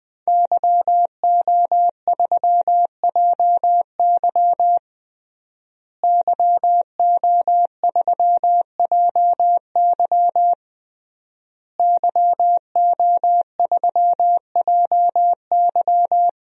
Morse code, heard as a keyed tone, about 21 wpm, 700 Hz. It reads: YO3JY YO3JY YO3JY
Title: DXCC-100 Call Signs - 5 Characters - Repeated 3x - 20wpm